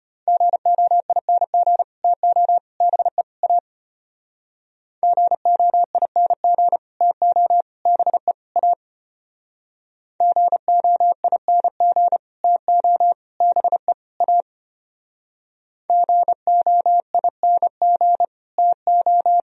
Morse code, heard as a keyed tone, about 32 wpm, 700 Hz. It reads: GOING TO BE A GOING TO BE A GOING TO BE A GOING TO